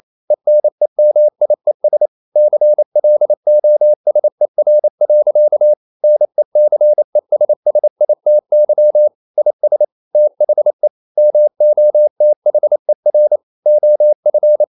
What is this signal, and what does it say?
Morse code, heard as a keyed tone, about 28 wpm, 600 Hz. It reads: ENEMIES CLOSER. NECESSITY IS THE MOTHER OF
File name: CW / Practice / EnglishSayings28